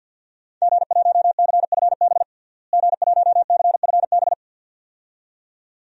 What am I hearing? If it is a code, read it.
Morse code, heard as a keyed tone, about 52 wpm, 700 Hz. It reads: G1CFB G1CFB
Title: DXCC-100 Call Signs - 5 Characters - Repeated 2x - 50wpm